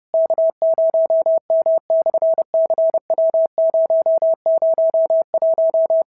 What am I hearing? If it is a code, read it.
Morse code, heard as a keyed tone, about 30 wpm, 690 Hz. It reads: K0M/CW001
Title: SOTA QSO Practice 30wpm